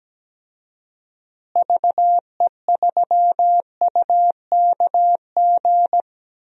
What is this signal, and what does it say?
Morse code, heard as a keyed tone, about 17 wpm, 700 Hz. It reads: VE3UKG
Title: DXCC-100 Call Signs - 6 Characters - Repeated 3x - 17wpm